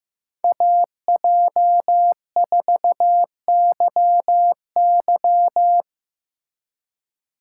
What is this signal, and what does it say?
Morse code, heard as a keyed tone, about 15 wpm, 700 Hz. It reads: AJ4YY
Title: USA Call Signs - 5 Characters - 3x Character Spacing - 15wpm